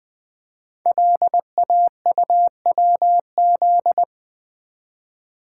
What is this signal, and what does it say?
Morse code, heard as a keyed tone, about 20 wpm, 700 Hz. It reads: LAUWZ
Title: Code Groups - 5 Letter:Numbers - 20wpm